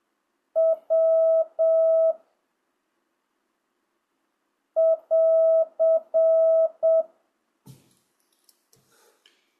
Morse code, heard as a keyed tone, about 7 wpm, 1300 Hz. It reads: W <AR>
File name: jh8cbh / 2025cw